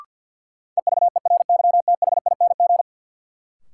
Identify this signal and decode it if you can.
Morse code, heard as a keyed tone, about 48 wpm, 700 Hz. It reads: EVERYTHING
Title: Sentences from Top 700 Words - New Word Drill - 50wpm